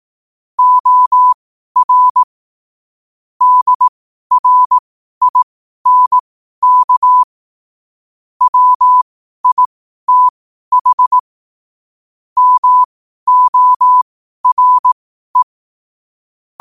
Morse code, heard as a keyed tone, about 18 wpm, spaced as for 12 wpm, 1000 Hz. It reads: OR DRINK WITH MORE
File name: qotdfiles / 12wpm